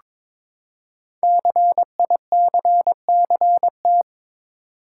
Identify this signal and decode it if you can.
Morse code, heard as a keyed tone, about 23 wpm, 700 Hz. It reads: CICCT
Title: Code Groups - 5 Letters - 22wpm